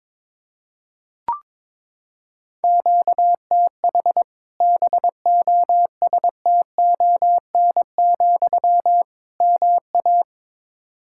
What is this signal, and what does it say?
Morse code, heard as a keyed tone, about 22 wpm, 700 Hz. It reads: QTH BOSTON, MA
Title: Practice QSO - Exchange 1 - Beginning - 22wpm